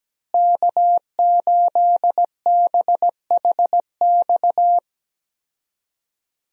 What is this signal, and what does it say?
Morse code, heard as a keyed tone, about 17 wpm, 700 Hz. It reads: K8BHX
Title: USA Call Signs - 5 Characters - Repeated 3x - 17wpm